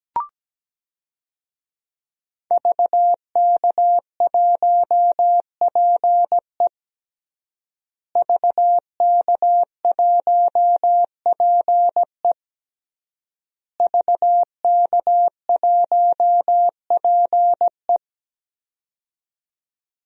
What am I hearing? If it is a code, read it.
Morse code, heard as a keyed tone, about 17 wpm, 700 Hz. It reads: VK1PE VK1PE VK1PE
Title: DXCC-100 Call Signs - 5 Characters - Repeated 3x - 17wpm